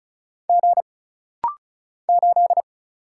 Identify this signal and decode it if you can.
Morse code, heard as a keyed tone, about 34 wpm, 700 Hz. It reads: G 8
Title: Single Letter-Number Alt - Rapid-Fire 35wpm